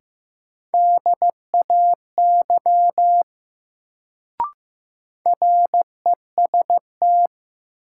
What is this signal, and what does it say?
Morse code, heard as a keyed tone, about 15 wpm, 700 Hz. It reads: DAY REST
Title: Sets of 2 Words - New Word Drill - 15wpm